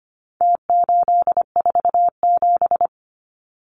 Morse code, heard as a keyed tone, about 25 wpm, 700 Hz. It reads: T847
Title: Numbers - Contest Style - 4-Digit 25wpm